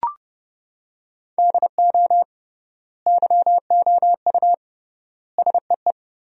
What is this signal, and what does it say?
Morse code, heard as a keyed tone, about 30 wpm, 700 Hz. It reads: DO YOU SEE
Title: 3x Word-Spacing - Sentences from Top 100 Words - 30wpm